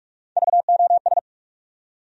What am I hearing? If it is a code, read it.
Morse code, heard as a keyed tone, about 42 wpm, 700 Hz. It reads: VOS
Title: USA Call Sign Postfixes - 3 Letters 45wpm